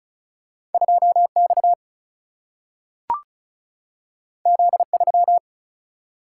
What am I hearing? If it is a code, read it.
Morse code, heard as a keyed tone, about 35 wpm, 700 Hz. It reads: 2X Z3
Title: Single-Letter Single-Number 35wpm